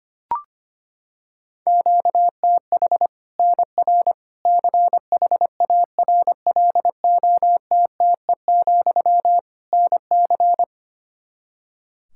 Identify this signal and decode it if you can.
Morse code, heard as a keyed tone, about 25 wpm, 700 Hz. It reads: QTH NR CHARLOTTE, NC
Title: Practice QSO - Exchange 1 - Beginning - 25wpm